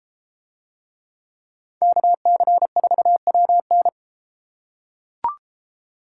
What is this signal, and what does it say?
Morse code, heard as a keyed tone, about 31 wpm, 700 Hz. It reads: KC4WN